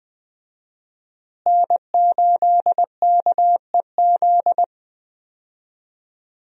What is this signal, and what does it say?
Morse code, heard as a keyed tone, about 20 wpm, 700 Hz. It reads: N8KEZ
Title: USA Call Signs - 5 Characters - Repeated 2x - 20wpm